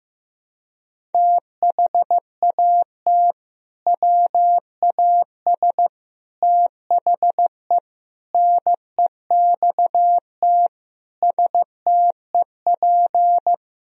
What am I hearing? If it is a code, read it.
Morse code, heard as a keyed tone, about 16 wpm, 700 Hz. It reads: THAT WAS THE NEXT STEP
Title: Sentences from Top 1000 Words - Review Part 4 - 15wpm